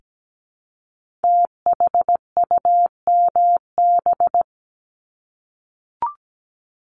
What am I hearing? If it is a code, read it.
Morse code, heard as a keyed tone, about 17 wpm, 700 Hz. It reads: THUMB
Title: Multi-letter Phonemes Repeated 3x 17wpm